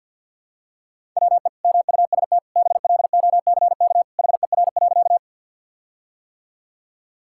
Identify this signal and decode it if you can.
Morse code, heard as a keyed tone, about 50 wpm, 700 Hz. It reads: WE MUST BLOCK HER.
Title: Sentences from Top 1600 Words 50wpm